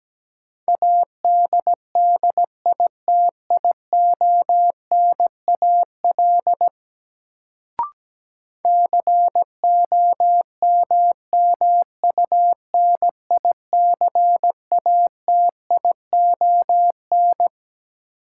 Morse code, heard as a keyed tone, about 17 wpm, 700 Hz. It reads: ADDITIONAL COMMUNICATION
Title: Sentences from Top 1400 Words - New Word Drill Alt - 17wpm